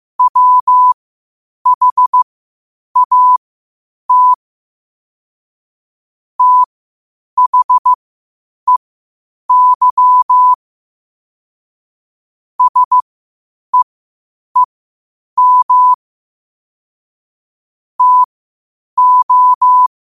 Morse code, heard as a keyed tone, about 15 wpm, spaced as for 8 wpm, 1000 Hz. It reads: WHAT THEY SEEM TO
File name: qotdfiles / 8wpm